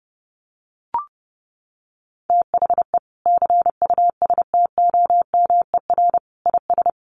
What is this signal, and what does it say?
Morse code, heard as a keyed tone, about 30 wpm, 700 Hz. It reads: THE CUSTOMER IS
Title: Sentences from Top 1100 Words 30wpm